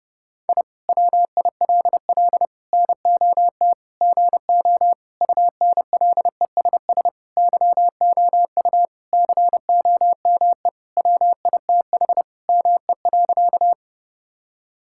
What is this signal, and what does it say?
Morse code, heard as a keyed tone, about 32 wpm, 700 Hz. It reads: I WILL NOT GO UNLESS YOU COME WITH ME.
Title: Sentences from Top 1000 Words - Review Part 6 - 30wpm